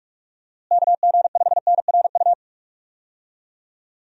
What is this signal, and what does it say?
Morse code, heard as a keyed tone, about 45 wpm, 700 Hz. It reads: KG5NRU
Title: USA Call Signs - 6 Characters - Repeated 3x - 45wpm